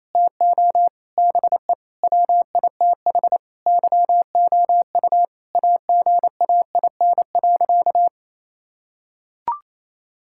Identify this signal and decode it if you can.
Morse code, heard as a keyed tone, about 28 wpm, 700 Hz. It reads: TO BE WITH YOU AGAIN.
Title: Sentences from Top 1600 Words 28wpm